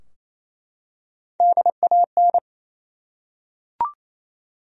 Morse code, heard as a keyed tone, about 27 wpm, 700 Hz. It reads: DAN
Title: USA Common Names - Max 4 Letters - 28wpm